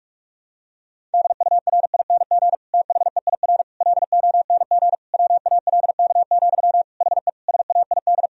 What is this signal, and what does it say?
Morse code, heard as a keyed tone, about 46 wpm, 700 Hz. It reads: DURING THEIR LONG WALK, HE SAID
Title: Sentences from Top 1000 Words - Review Part 2 - 45wpm